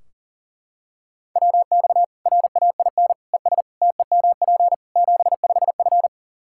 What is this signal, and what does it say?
Morse code, heard as a keyed tone, about 40 wpm, 700 Hz. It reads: WX RAIN ES TEMP 75F
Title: Practice QSO - Exchange 2 - Info - 40wpm